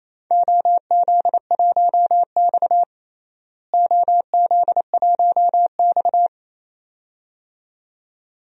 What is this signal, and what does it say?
Morse code, heard as a keyed tone, about 28 wpm, 700 Hz. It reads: OZ1X OZ1X
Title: DXCC-100 Call Signs - 4 Characters - Repeated 2x - 28wpm